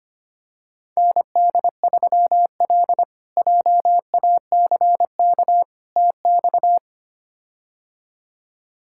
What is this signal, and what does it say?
Morse code, heard as a keyed tone, about 25 wpm, 700 Hz. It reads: ND3L JACK TX
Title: K1USN SST QSO Practice - 25wpm